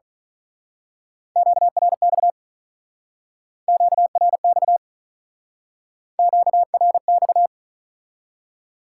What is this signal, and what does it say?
Morse code, heard as a keyed tone, about 40 wpm, 700 Hz. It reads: QRX QRX QRX
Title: Speed-Racing Top 180 QSO Elements 35wpm